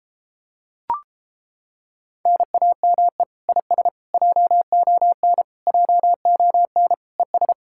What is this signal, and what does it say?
Morse code, heard as a keyed tone, about 31 wpm, 700 Hz. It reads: NAME IS JON JON ES